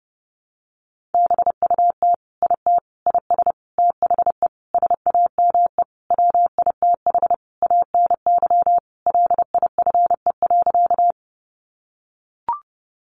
Morse code, heard as a keyed tone, about 30 wpm, 700 Hz. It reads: BUT IT IS THE SAME WITH ANY LIFE.
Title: Sentences from Top 300 Words 30wpm